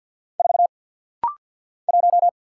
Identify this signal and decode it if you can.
Morse code, heard as a keyed tone, about 50 wpm, 700 Hz. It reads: 4 1
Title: Single Letter-Number - Rapid-Fire 50wpm